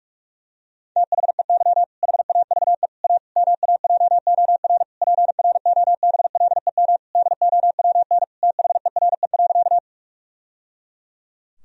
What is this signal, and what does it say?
Morse code, heard as a keyed tone, about 44 wpm, 700 Hz. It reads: THEY HAVE A MAJOR PROBLEM DOWN THERE.
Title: Sentences from Top 1000 Words - Review Part 4 - 45wpm